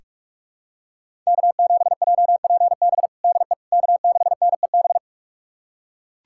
Morse code, heard as a keyed tone, about 44 wpm, 700 Hz. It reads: K7JPB DE K6NEB
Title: USA Callsign DE Callsign 45wpm